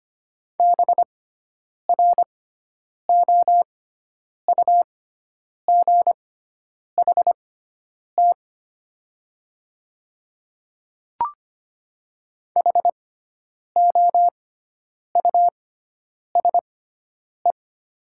Morse code, heard as a keyed tone, about 25 wpm, spaced as for 9 wpm, 700 Hz. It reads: BROUGHT HOUSE
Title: 6x Character Spacing - Top 500 Words - 25wpm